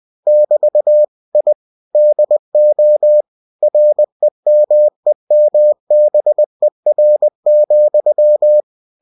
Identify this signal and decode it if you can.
Morse code, heard as a keyed tone, about 20 wpm, 600 Hz. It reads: <BT> I DO REMEMBER,
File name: cw-books / winnie-the-pooh / winnie-20wpm-page-0011